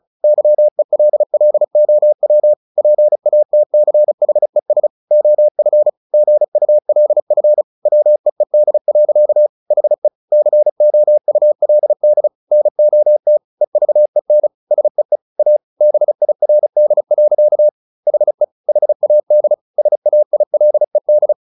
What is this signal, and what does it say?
Morse code, heard as a keyed tone, about 34 wpm, 600 Hz. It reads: YELLOW PATCHES OF GULF WEED. HE COULD NOT EVEN SEE A BIRD. HE HAD SAILED
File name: the old man and the sea 2 35wpm